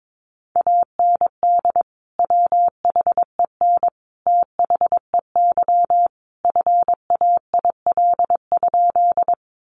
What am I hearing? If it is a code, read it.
Morse code, heard as a keyed tone, about 22 wpm, 700 Hz. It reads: AND WHEN THEY FAIL?
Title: Sentences from Top 1100 Words 22wpm